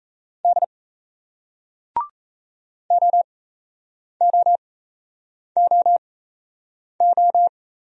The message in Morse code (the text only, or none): D OOOO